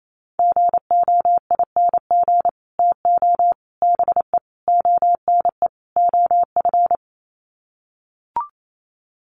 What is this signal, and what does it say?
Morse code, heard as a keyed tone, about 28 wpm, 700 Hz. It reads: GOING TO BE ONE OF
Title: Sets of 5 Words Encore2 28wpm